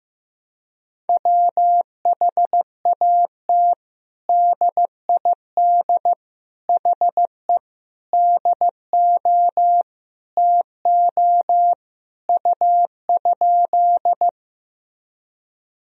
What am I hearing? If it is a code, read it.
Morse code, heard as a keyed tone, about 15 wpm, 700 Hz. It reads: WHAT DID HE DO TO U?
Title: Sentences from Top 100 Words and Abbreviations 15wpm